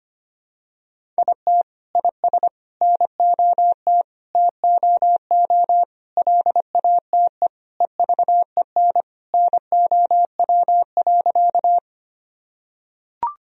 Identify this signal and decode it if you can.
Morse code, heard as a keyed tone, about 25 wpm, 700 Hz. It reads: IT IS NOT TOO LATE EVEN NOW.
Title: Sentences from Top 1000 Words - Review Part 1 - 25wpm